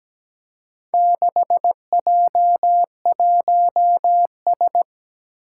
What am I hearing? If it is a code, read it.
Morse code, heard as a keyed tone, about 17 wpm, 700 Hz. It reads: 6J1S
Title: Four Challenging Letter-Number Combinations 17wpm